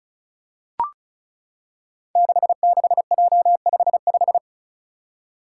Morse code, heard as a keyed tone, about 36 wpm, 700 Hz. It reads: 66J55
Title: Five Challenging Letter-Number Combinations 35wpm